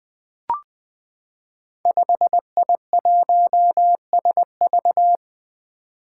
Morse code, heard as a keyed tone, about 20 wpm, 700 Hz. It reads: 5I1SV
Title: Five Challenging Letter-Number Combinations 20wpm